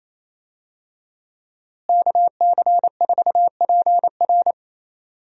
Morse code, heard as a keyed tone, about 27 wpm, 700 Hz. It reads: KC4PR
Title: USA Call Signs - 5 Characters - Repeated 2x - 28wpm